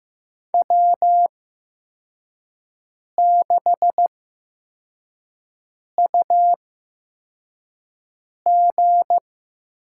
Morse code, heard as a keyed tone, about 15 wpm, spaced as for 4 wpm, 700 Hz. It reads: W6UG
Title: USA Call Signs - 4 Characters - 8x Character Spacing - 15wpm